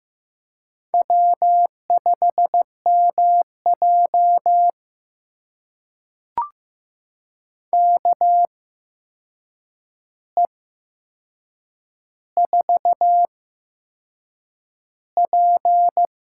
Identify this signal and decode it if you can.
Morse code, heard as a keyed tone, about 15 wpm, spaced as for 6 wpm, 700 Hz. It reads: W5MJ KE4P
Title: USA Call Signs - 4 Characters - 8x Character Spacing - 15wpm